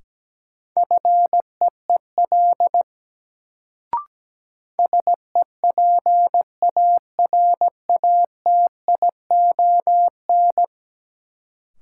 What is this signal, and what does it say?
Morse code, heard as a keyed tone, about 17 wpm, 700 Hz. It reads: FEEL SEPARATION
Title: Sets of 5 Words - New Word Drill - 17wpm